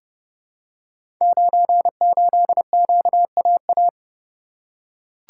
Morse code, heard as a keyed tone, about 30 wpm, 700 Hz. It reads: 98QAA
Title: Code Groups - 5 Letter:Numbers - 30wpm